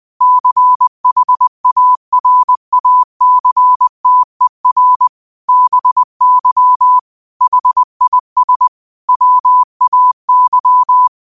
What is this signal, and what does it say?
Morse code, heard as a keyed tone, about 20 wpm, 1000 Hz. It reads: CHARACTER BY HIS WAY